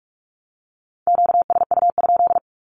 Morse code, heard as a keyed tone, about 46 wpm, 700 Hz. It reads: QSU?
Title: Q-Codes 45wpm